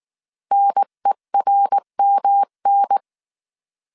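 Morse code, heard as a keyed tone, about 19 wpm, 800 Hz. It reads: NERMN
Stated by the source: Morse code dictation at 18 wpm from the IW3SGT QRS Morse code training course, five-character groups of 300 characters.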